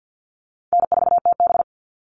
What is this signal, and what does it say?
Morse code, heard as a keyed tone, about 50 wpm, 700 Hz. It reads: N4TB
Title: Easy Call Signs - 50wpm